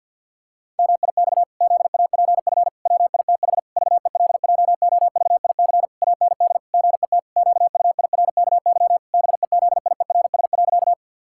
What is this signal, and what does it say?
Morse code, heard as a keyed tone, about 50 wpm, 700 Hz. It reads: MIX ZAPF WITH VELJOVIC AND GET QUIRKY BEZIERS.
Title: Pangram Challenge v2 - 50wpm